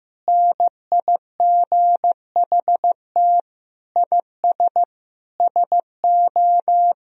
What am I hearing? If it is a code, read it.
Morse code, heard as a keyed tone, about 15 wpm, 700 Hz. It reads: NIGHT IS SO
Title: Sentences from Top 500 Words and Abbreviations 15wpm